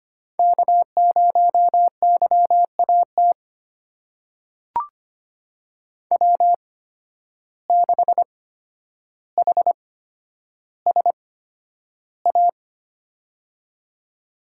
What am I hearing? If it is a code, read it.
Morse code, heard as a keyed tone, about 25 wpm, spaced as for 11 wpm, 700 Hz. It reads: K0YAT W6HSA
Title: USA Call Signs - 5 Characters - 8x Character Spacing - 25wpm